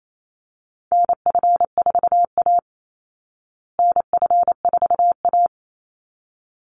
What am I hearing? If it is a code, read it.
Morse code, heard as a keyed tone, about 28 wpm, 700 Hz. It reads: NF4A NF4A
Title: Call Signs Repeated 3x - 28wpm